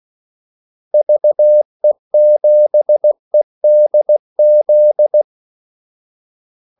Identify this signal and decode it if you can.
Morse code, heard as a keyed tone, about 16 wpm, 600 Hz. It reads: VE7EDZ
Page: Kaldesignaler 16 wpm